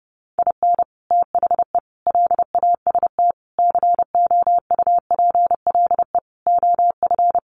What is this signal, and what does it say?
Morse code, heard as a keyed tone, about 30 wpm, 700 Hz. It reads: IN THE LAST COUPLE OF
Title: Sets of 5 Words 30wpm